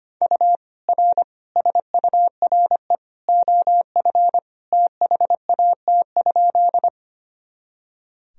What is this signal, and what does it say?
Morse code, heard as a keyed tone, about 25 wpm, 700 Hz. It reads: U R SURE OF THAT?
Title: Sentences from Top 300 Words and Abbreviations 25wpm